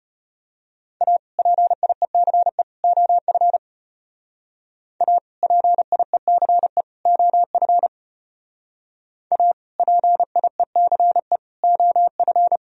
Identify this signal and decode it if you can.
Morse code, heard as a keyed tone, about 31 wpm, 700 Hz. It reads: A PIECE OF A PIECE OF A PIECE OF